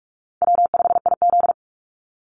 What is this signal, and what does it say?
Morse code, heard as a keyed tone, about 44 wpm, 700 Hz. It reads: W5IZ
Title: Call Signs - 45wpm